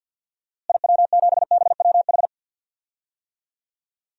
Morse code, heard as a keyed tone, about 53 wpm, 700 Hz. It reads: IW7BWH
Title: DXCC-100 Call Signs - 6 Characters - Repeated 2x - 50wpm